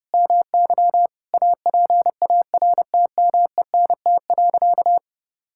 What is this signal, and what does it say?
Morse code, heard as a keyed tone, about 30 wpm, 700 Hz. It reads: MY APARTMENT.
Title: Sentences from Top 1000 Words - Review Part 6 - 30wpm